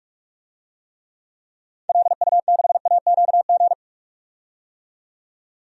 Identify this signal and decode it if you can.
Morse code, heard as a keyed tone, about 45 wpm, 700 Hz. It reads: LU6AQG
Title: DXCC-100 Call Signs - 6 Characters - 45wpm